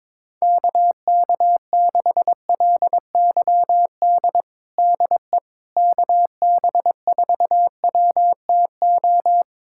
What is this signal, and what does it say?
Morse code, heard as a keyed tone, about 22 wpm, 700 Hz. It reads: KK6LYD DE KB4WTO
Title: USA Callsign DE Callsign - 6 Characters - 22wpm